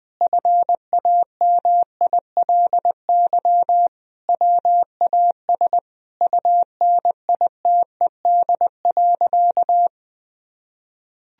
Morse code, heard as a keyed tone, about 21 wpm, 700 Hz. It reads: FAMILY WAS UNITED.
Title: Sentences from Top 1000 Words - Review Part 4 - 20wpm